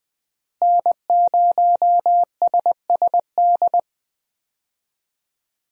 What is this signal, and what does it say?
Morse code, heard as a keyed tone, about 20 wpm, 700 Hz. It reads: N0SSD
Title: USA Call Signs - 5 Characters - Repeated 3x - 20wpm